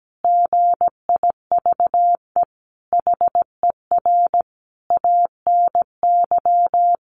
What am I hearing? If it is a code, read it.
Morse code, heard as a keyed tone, about 17 wpm, 700 Hz. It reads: GIVE HER ANY
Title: Sentences from Top 1500 Words 17wpm